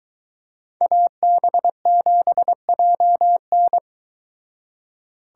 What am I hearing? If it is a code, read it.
Morse code, heard as a keyed tone, about 22 wpm, 700 Hz. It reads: AB7JN